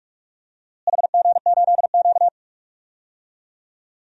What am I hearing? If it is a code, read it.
Morse code, heard as a keyed tone, about 44 wpm, 700 Hz. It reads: HG8Q
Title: DXCC-100 Call Signs - 4 Characters - Repeated 3x - 45wpm